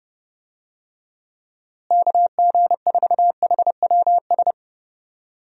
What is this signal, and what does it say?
Morse code, heard as a keyed tone, about 30 wpm, 700 Hz. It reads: KG4HWS
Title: USA Call Signs - 6 Characters - Repeated 2x - 30wpm